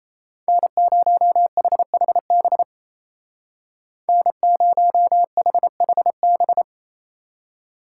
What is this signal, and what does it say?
Morse code, heard as a keyed tone, about 34 wpm, 700 Hz. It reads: N0HHB N0HHB
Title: Speed Racing - USA 1x3 Call Signs 28wpm